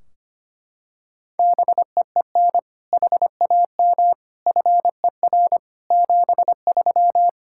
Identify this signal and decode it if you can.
Morse code, heard as a keyed tone, about 25 wpm, 700 Hz. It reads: BEEN HAM FER 73
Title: Practice QSO - Exchange 3 - More Info - 25wpm